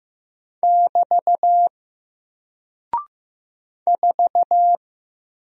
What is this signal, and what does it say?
Morse code, heard as a keyed tone, about 15 wpm, 700 Hz. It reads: <BT> 4
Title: Single Character (Letter:Number:Punctuation:Prosign) Alt 15wpm